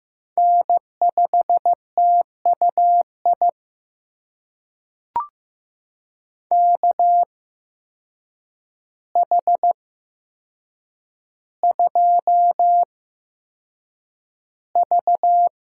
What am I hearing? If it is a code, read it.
Morse code, heard as a keyed tone, about 15 wpm, spaced as for 7 wpm, 700 Hz. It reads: N5TUI KH2V
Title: USA Call Signs - 5 Characters - 8x Character Spacing - 15wpm